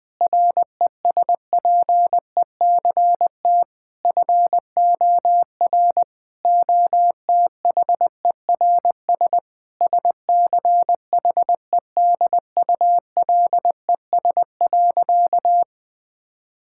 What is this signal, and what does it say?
Morse code, heard as a keyed tone, about 20 wpm, 700 Hz. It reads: RESPECT FOR OTHERS SCHEDULES.
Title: Difficult Sentences with Hints - 20wpm